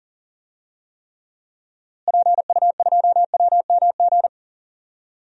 Morse code, heard as a keyed tone, about 39 wpm, 700 Hz. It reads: PU2WMG
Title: DXCC-100 Call Signs - 6 Characters - Repeated 2x - 40wpm